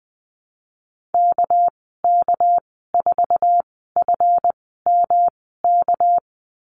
Morse code, heard as a keyed tone, about 20 wpm, spaced as for 15 wpm, 700 Hz. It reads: KK4FMK
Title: 200 Lesson 32g - Letter k - Call Signs 20wpm